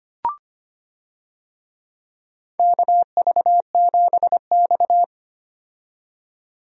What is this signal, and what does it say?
Morse code, heard as a keyed tone, about 25 wpm, 700 Hz. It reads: KV7X
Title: USA Call Signs - 4 Characters - 25wpm